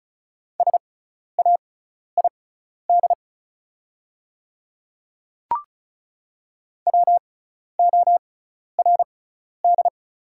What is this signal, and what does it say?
Morse code, heard as a keyed tone, about 36 wpm, spaced as for 12 wpm, 700 Hz. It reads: SAID WORD
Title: 6x Character Spacing - Top 100 Words - 35wpm